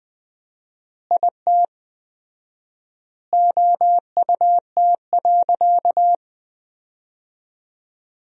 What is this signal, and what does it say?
Morse code, heard as a keyed tone, about 19 wpm, 700 Hz. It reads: IT OUT.
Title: 4x Word-Spacing - Sentences from Top 300 Words - 20wpm